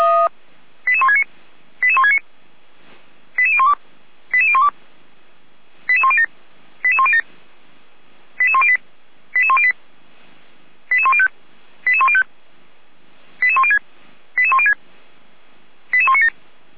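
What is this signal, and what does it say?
Five-tone selective call (ZVEI), heard as quick runs of five tones, tones of about 70 ms. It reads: 88169 88169 88012 88012 88107 88107 88108 88108 88105 88105 88106 88106 88197
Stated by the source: ZVEI five-tone selective call received on SDR.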